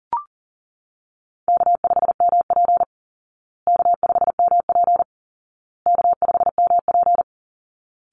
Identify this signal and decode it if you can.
Morse code, heard as a keyed tone, about 40 wpm, 700 Hz. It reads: K5MP K5MP K5MP
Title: Call Signs Repeated 2x - 40wpm